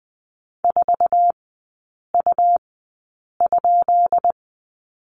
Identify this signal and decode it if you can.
Morse code, heard as a keyed tone, about 20 wpm, spaced as for 8 wpm, 700 Hz. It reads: 4U?
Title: 133 Lesson 22e - Question Mark - Three Characters 20wpm